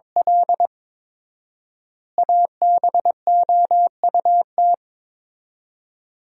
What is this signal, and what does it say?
Morse code, heard as a keyed tone, about 22 wpm, 700 Hz. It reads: L ABOUT
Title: 4x Word-Spacing - Sentences from Top 200 Words - 22wpm